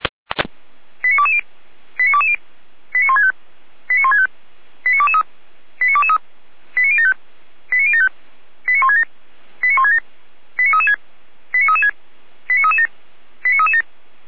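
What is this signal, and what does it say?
Five-tone selective call (ZVEI), heard as quick runs of five tones, tones of about 70 ms. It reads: 89220 89220 89165 89165 89223 89223 89075 89075 89167 89167 89336 89336 89337 89337
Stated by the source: Five-tone ZVEI selective call 89014 received on SDR.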